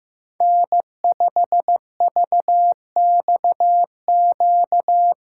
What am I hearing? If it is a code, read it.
Morse code, heard as a keyed tone, about 15 wpm, 700 Hz. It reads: N5VXQ
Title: USA Callsign DE Callsign 15wpm